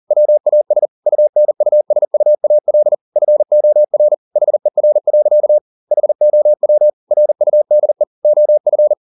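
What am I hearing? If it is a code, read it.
Morse code, heard as a keyed tone, about 40 wpm, 600 Hz. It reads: WAS UNUSUAL FOR HER. HOW RUDE OF
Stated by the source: Morse code at 40 wpm, practice text, CW med Gnister.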